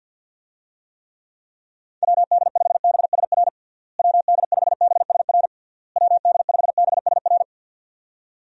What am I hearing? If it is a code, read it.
Morse code, heard as a keyed tone, about 46 wpm, 700 Hz. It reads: WD5BSR WD5BSR WD5BSR
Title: USA Call Signs - 6 Characters - Repeated 3x - 50wpm